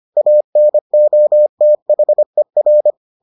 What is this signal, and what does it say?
Morse code, heard as a keyed tone, about 25 wpm, 600 Hz. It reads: ANOTHER